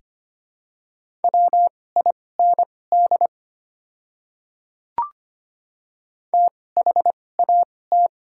Morse code, heard as a keyed tone, about 25 wpm, spaced as for 18 wpm, 700 Hz. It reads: WIND THAT
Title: 2x Character Spacing - Top 500 Words - 25wpm